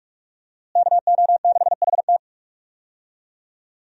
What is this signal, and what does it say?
Morse code, heard as a keyed tone, about 45 wpm, 700 Hz. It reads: KO6HT
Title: USA Call Signs - 5 Characters - Repeated 2x - 45wpm